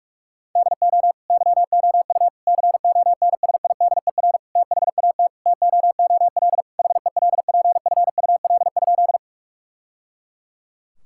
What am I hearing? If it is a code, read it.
Morse code, heard as a keyed tone, about 46 wpm, 700 Hz. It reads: DO YOU CONSIDER THAT TOOL HELPFUL?
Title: Sentences from Top 1000 Words - Review Part 6 - 45wpm